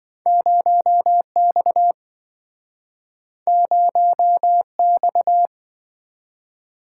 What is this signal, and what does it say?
Morse code, heard as a keyed tone, about 27 wpm, 700 Hz. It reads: 0X 0X